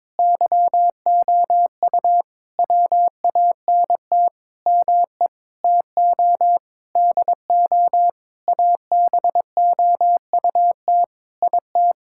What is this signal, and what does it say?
Morse code, heard as a keyed tone, about 22 wpm, 700 Hz. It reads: YOU WANT ME TO DO ABOUT IT
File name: Sentences with 7-Word Phrases - 22wpm